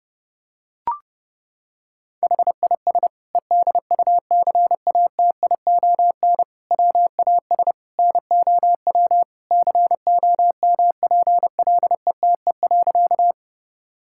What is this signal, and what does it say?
Morse code, heard as a keyed tone, about 30 wpm, 700 Hz. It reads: HIS EDUCATION WAS NOW COMPLETE.
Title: Sentences from Top 1000 Words - Review Part 6 - 30wpm